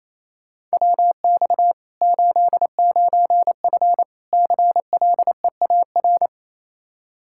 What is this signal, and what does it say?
Morse code, heard as a keyed tone, about 28 wpm, 700 Hz. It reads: WX 89F CLEAR
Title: Practice QSO - Exchange 2 - Info - 28wpm